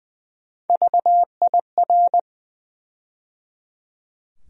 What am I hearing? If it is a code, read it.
Morse code, heard as a keyed tone, about 20 wpm, 700 Hz. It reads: VIR
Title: USA Call Sign Postfixes - 3 Letters 20wpm